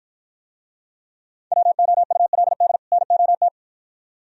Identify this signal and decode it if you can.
Morse code, heard as a keyed tone, about 53 wpm, 700 Hz. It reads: WOULD NOT